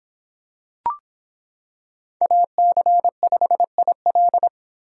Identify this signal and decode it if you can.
Morse code, heard as a keyed tone, about 26 wpm, 700 Hz. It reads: AC5IL